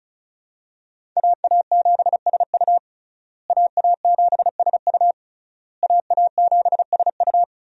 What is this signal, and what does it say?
Morse code, heard as a keyed tone, about 36 wpm, 700 Hz. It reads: AA7SU AA7SU AA7SU
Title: USA Call Signs - 5 Characters - Repeated 3x - 35wpm